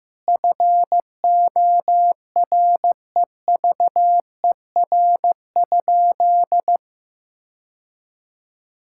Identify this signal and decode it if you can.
Morse code, heard as a keyed tone, about 15 wpm, 700 Hz. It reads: FOREVER?
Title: Sentences from Top 1600 Words 15wpm